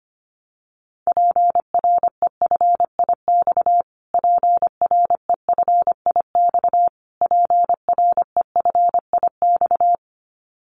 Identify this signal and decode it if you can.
Morse code, heard as a keyed tone, about 25 wpm, 700 Hz. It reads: PREFIX PREFIX PREFIX
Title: Callsign Prefixes for Top 25 DXCC - Repeated 3x - 25wpm